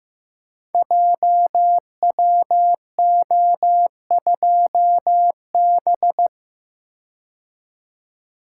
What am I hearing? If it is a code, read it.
Morse code, heard as a keyed tone, about 15 wpm, 700 Hz. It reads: JWO2B
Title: Code Groups - 5 Letter:Numbers - 15wpm